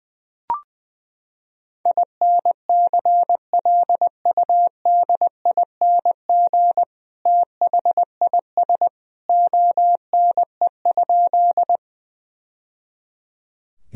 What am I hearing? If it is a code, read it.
Morse code, heard as a keyed tone, about 20 wpm, 700 Hz. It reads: INCLUDING THIS ONE?
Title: Sentences from Top 1000 Words - Review Part 4 - 20wpm